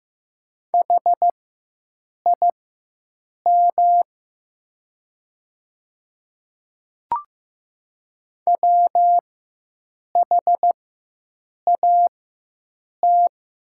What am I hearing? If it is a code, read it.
Morse code, heard as a keyed tone, about 15 wpm, spaced as for 7 wpm, 700 Hz. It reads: HIM WHAT
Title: 4x Character Spacing - Top 100 Words - 15wpm